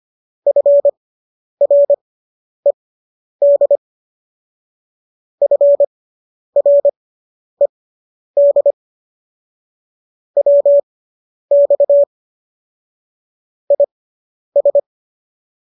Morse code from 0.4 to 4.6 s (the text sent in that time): FRED